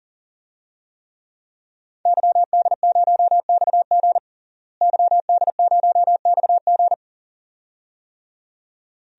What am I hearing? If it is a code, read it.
Morse code, heard as a keyed tone, about 37 wpm, 700 Hz. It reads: YD0XG YD0XG
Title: DXCC-100 Call Signs - 5 Characters - Repeated 2x - 40wpm